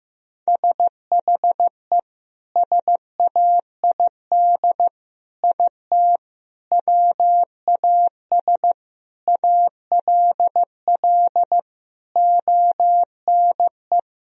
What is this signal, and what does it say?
Morse code, heard as a keyed tone, about 15 wpm, 700 Hz. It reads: SHE SAID IT WAS ALL ONE
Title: Sentences from Top 1000 Words - Review Part 2 - 15wpm